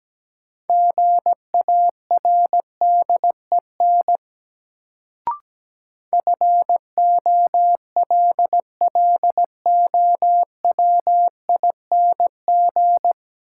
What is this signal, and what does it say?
Morse code, heard as a keyed tone, about 17 wpm, 700 Hz. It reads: GARDEN FOLLOWING
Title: Sentences from Top 700 Words - New Word Drill - 17wpm